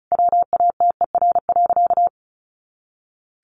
Morse code, heard as a keyed tone, about 34 wpm, 700 Hz. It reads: WATER.
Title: Additional Easy Phrases 35wpm